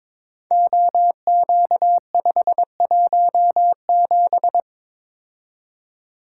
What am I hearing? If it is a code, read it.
Morse code, heard as a keyed tone, about 22 wpm, 700 Hz. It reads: OQ517
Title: Code Groups - 5 Letter:Numbers - 22wpm